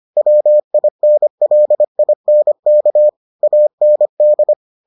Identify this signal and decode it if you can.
Morse code, heard as a keyed tone, about 25 wpm, 600 Hz. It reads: WINLINK AND